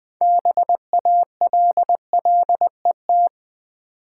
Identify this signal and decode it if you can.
Morse code, heard as a keyed tone, about 20 wpm, 700 Hz. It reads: BALLET
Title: Build A Word - 3 Letters At A Time - 6-Letter Words - 20wpm